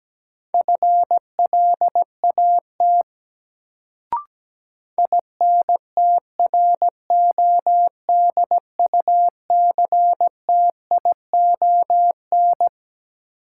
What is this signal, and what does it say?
Morse code, heard as a keyed tone, about 17 wpm, 700 Hz. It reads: FLAT INTRODUCTION
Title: Sentences from Top 1000 Words - New Word Drill - 17wpm